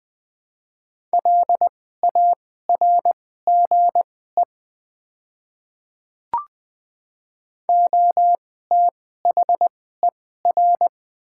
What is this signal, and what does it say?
Morse code, heard as a keyed tone, about 20 wpm, spaced as for 14 wpm, 700 Hz. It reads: LARGE OTHER
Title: 2x Character Spacing - Top 300 Words - 20wpm